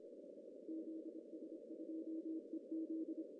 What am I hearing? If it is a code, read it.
Morse code, heard as a keyed tone, about 27 wpm, 320 Hz. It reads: ZS2EZ
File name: CW / cqww-cw-k5tr